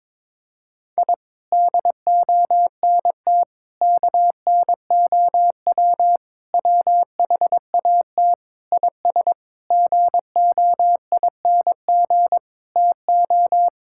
Morse code, heard as a keyed tone, about 21 wpm, 700 Hz. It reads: I DONT KNOW WHAT IS GOING TO
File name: Sentences with 7-Word Phrases - 22wpm